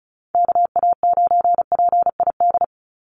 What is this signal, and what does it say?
Morse code, heard as a keyed tone, about 34 wpm, 700 Hz. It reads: KA9PID
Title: USA Call Signs 35wpm